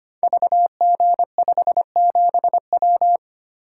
Morse code, heard as a keyed tone, about 25 wpm, 700 Hz. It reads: VG57W
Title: Code Groups - 5 Letter:Numbers - 25wpm